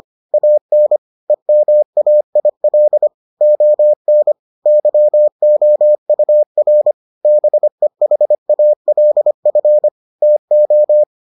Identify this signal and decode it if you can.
Morse code, heard as a keyed tone, about 25 wpm, 600 Hz. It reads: AN EMAIL ON YOUR BEHALF TO